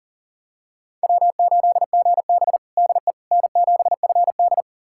Morse code, heard as a keyed tone, about 37 wpm, 700 Hz. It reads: W8GB DE N7FD
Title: USA Callsign DE Callsign - 4 Characters - 40wpm